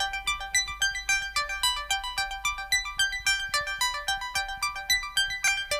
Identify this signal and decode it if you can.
Emergency Alert System (SAME) tones (header bursts), heard as harsh buzzing bursts. It is heard from 0.4 to 5.6 s.